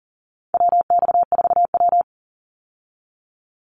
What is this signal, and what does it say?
Morse code, heard as a keyed tone, about 40 wpm, 700 Hz. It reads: WX4W
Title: Call Signs - 40wpm